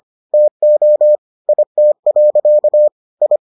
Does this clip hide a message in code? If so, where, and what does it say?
Yes: Morse code from 0.3 to 3.4 s, heard as a keyed tone, about 25 wpm, 600 Hz. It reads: TO IT. I